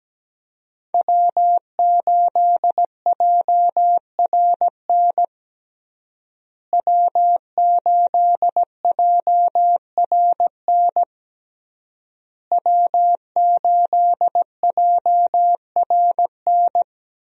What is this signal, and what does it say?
Morse code, heard as a keyed tone, about 17 wpm, 700 Hz. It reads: W8JRN W8JRN W8JRN
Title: USA Call Signs - 5 Characters - Repeated 3x - 17wpm